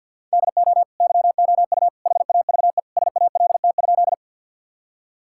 Morse code, heard as a keyed tone, about 50 wpm, 700 Hz. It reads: DO YOU HAVE SALT?
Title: Sentences from Top 1000 Words - Review Part 5 - 50wpm